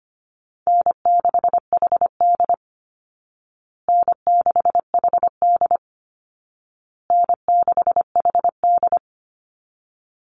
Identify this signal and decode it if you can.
Morse code, heard as a keyed tone, about 25 wpm, 700 Hz. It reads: N6HD N6HD N6HD
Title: Easy Call Signs Repeated 3x - 25wpm